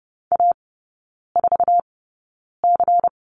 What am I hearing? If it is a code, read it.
Morse code, heard as a keyed tone, about 30 wpm, spaced as for 9 wpm, 690 Hz. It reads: A4C
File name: Lesson 16e - Letter c - Three Characters 30wpm